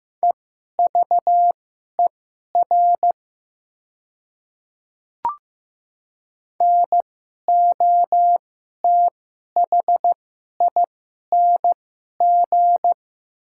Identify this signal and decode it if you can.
Morse code, heard as a keyed tone, about 15 wpm, spaced as for 11 wpm, 690 Hz. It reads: EVER NOTHING
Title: 2x Character Spacing - Top 500 Words - 15wpm